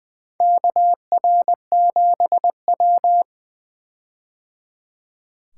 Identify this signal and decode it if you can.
Morse code, heard as a keyed tone, about 20 wpm, 700 Hz. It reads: KR7W
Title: K1USN SST QSO Practice - 20wpm